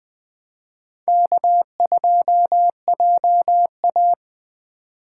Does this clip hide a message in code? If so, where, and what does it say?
Yes: Morse code from 1.1 to 4.1 s, heard as a keyed tone, about 19 wpm, 700 Hz. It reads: K2JA